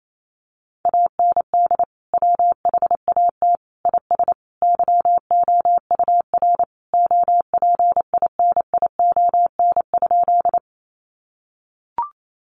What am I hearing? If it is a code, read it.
Morse code, heard as a keyed tone, about 28 wpm, 700 Hz. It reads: AND WHAT IS YOUR OPINION?
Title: Sentences from Top 1200 Words 28wpm